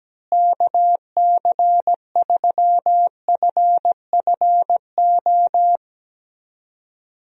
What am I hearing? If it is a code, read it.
Morse code, heard as a keyed tone, about 17 wpm, 700 Hz. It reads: KC3FFO
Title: USA Call Signs - 6 Characters - Repeated 3x - 17wpm